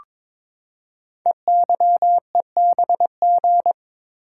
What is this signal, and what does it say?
Morse code, heard as a keyed tone, about 22 wpm, 700 Hz. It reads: EYEBG
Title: Code Groups - 5 Letters - 22wpm